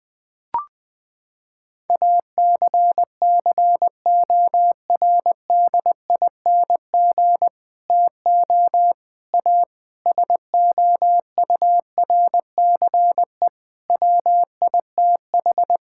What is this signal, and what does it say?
Morse code, heard as a keyed tone, about 20 wpm, 700 Hz. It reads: ACCORDING TO A SOURCE WITH
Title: Sets of 9 Words - 20wpm